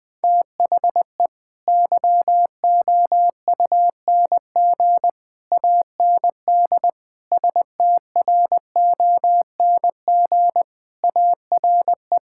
Morse code, heard as a keyed tone, about 21 wpm, 700 Hz. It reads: THE YOUNG AND STRONG ARE
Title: Sentences from Top 1000 Words - Review Part 5 - 20wpm